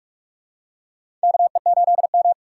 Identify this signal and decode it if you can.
Morse code, heard as a keyed tone, about 46 wpm, 700 Hz. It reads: KE8M
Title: K1USN SST QSO Practice - 45wpm